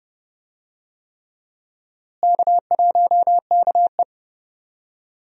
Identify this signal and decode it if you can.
Morse code, heard as a keyed tone, about 32 wpm, 700 Hz. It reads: K1KE
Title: USA Call Signs - 4 Characters - Repeated 3x - 30wpm